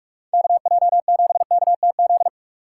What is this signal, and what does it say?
Morse code, heard as a keyed tone, about 46 wpm, 700 Hz. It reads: KJ7KTZ
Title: USA Call Signs - 6 Characters - 4x Character Spacing - 45wpm